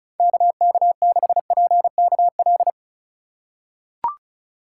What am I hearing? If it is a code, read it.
Morse code, heard as a keyed tone, about 36 wpm, 700 Hz. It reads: KK6PKL
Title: USA Call Signs - 6 Characters - 3x Character Spacing - 35wpm